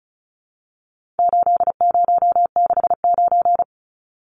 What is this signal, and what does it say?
Morse code, heard as a keyed tone, about 34 wpm, 700 Hz. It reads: 8069
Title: Numbers - 4-Digit 35wpm